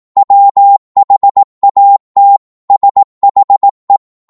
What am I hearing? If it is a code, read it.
Morse code, heard as a keyed tone, about 18 wpm, 800 Hz. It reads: WHAT SHE
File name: Morse / psg-18wpm